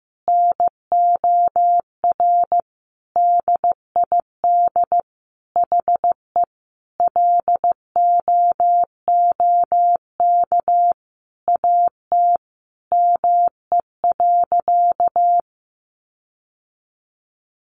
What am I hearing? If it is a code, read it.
Morse code, heard as a keyed tone, about 15 wpm, 700 Hz. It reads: NOR DID HE LOOK AT ME.
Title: Sentences from Top 1500 Words 15wpm